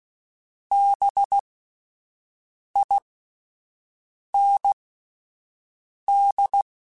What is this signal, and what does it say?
Morse code, heard as a keyed tone, about 16 wpm, spaced as for 5 wpm, 800 Hz. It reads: BIND
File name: Spaced Alphas / Spaced Alphas mp3s / Bs sp @10